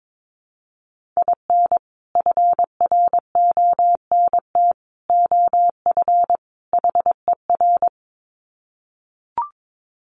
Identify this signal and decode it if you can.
Morse code, heard as a keyed tone, about 22 wpm, 700 Hz. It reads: IN FRONT OF HER
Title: Sets of 4 Words 22wpm